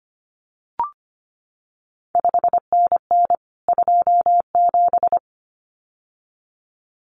Morse code, heard as a keyed tone, about 25 wpm, 700 Hz. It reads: 5NN 27
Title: CQ WW DX Contest Exchange Encore - 25wpm